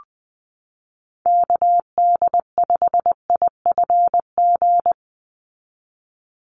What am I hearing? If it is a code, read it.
Morse code, heard as a keyed tone, about 20 wpm, 700 Hz. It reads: KD5IFG
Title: USA Call Signs 20wpm